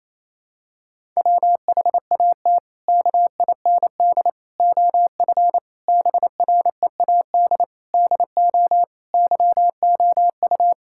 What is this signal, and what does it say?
Morse code, heard as a keyed tone, about 28 wpm, 700 Hz. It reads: WHAT KIND OF BREAD DO YOU
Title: Sentences from Top 1600 Words 28wpm